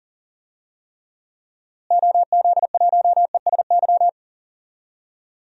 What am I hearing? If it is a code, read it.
Morse code, heard as a keyed tone, about 40 wpm, 700 Hz. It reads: OZ1ESY
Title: DXCC-100 Call Signs - 6 Characters - Repeated 2x - 40wpm